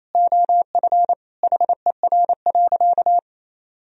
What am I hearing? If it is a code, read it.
Morse code, heard as a keyed tone, about 28 wpm, 700 Hz. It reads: OF HER.
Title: Sentences from Top 1000 Words - Review Part 3 - 28wpm